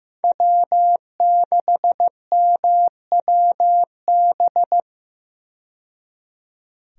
Morse code, heard as a keyed tone, about 16 wpm, 700 Hz. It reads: W6MWB
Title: USA Call Signs - 5 Characters - 15wpm